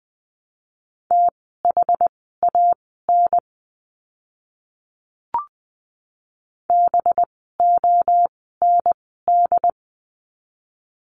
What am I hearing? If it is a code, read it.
Morse code, heard as a keyed tone, about 20 wpm, spaced as for 14 wpm, 700 Hz. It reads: THAN BOND
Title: 185 Lesson 30f - Letter b - Words 20wpm